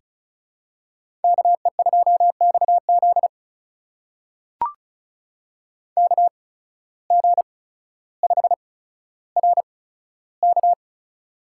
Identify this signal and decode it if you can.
Morse code, heard as a keyed tone, about 34 wpm, spaced as for 15 wpm, 700 Hz. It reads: KE2XZ KG5RK
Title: USA Call Signs - 5 Characters - 8x Character Spacing - 35wpm